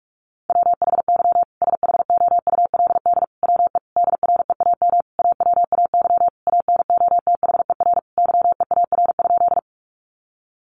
Morse code, heard as a keyed tone, about 44 wpm, 700 Hz. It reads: WHY SHOULD WE DREAM AWAY ANOTHER YEAR?
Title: Sentences from Top 1500 Words 45wpm